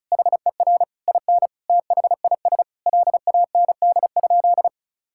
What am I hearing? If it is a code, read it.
Morse code, heard as a keyed tone, about 34 wpm, 700 Hz. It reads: HER IN THIS LAND?
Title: Sentences from Top 1600 Words 35wpm